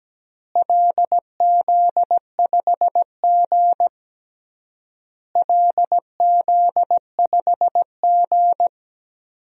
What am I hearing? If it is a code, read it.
Morse code, heard as a keyed tone, about 17 wpm, 700 Hz. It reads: LZ5G LZ5G
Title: DXCC-100 Call Signs - 4 Characters - Repeated 3x - 17wpm